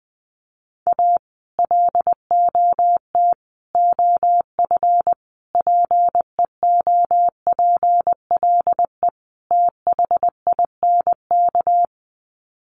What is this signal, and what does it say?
Morse code, heard as a keyed tone, about 20 wpm, 700 Hz. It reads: A LOT OF PEOPLE THINK
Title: Sets of 5 Words Encore 20wpm